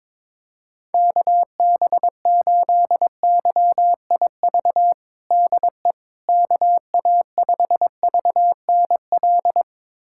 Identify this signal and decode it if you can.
Morse code, heard as a keyed tone, about 22 wpm, 700 Hz. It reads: KB8YIV DE KA5VNL
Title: USA Callsign DE Callsign - 6 Characters - 22wpm